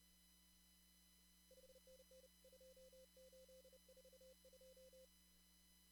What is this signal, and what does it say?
Morse code, heard as a keyed tone, about 31 wpm, 520 Hz. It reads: 5NN 2842
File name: CW / kt5j wpx cw-op-k5pi